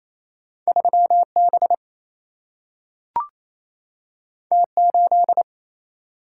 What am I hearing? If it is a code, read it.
Morse code, heard as a keyed tone, about 28 wpm, 700 Hz. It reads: 3 B T 8
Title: Single-Letter Single-Number 28wpm